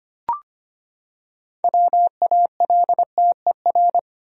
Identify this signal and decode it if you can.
Morse code, heard as a keyed tone, about 25 wpm, 700 Hz. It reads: WALTER
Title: USA Common Names - 25wpm